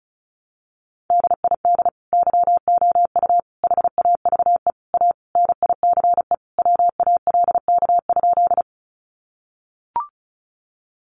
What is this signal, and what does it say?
Morse code, heard as a keyed tone, about 34 wpm, 700 Hz. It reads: DID YOU HAVE A NICE WALK?
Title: Sentences from Top 1100 Words 35wpm